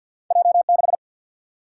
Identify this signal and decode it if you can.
Morse code, heard as a keyed tone, about 48 wpm, 700 Hz. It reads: J 6
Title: Single-Letter Single-Number 50wpm